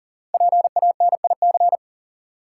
Morse code, heard as a keyed tone, about 38 wpm, 700 Hz. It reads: PANIC
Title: Five Letter Words - 40wpm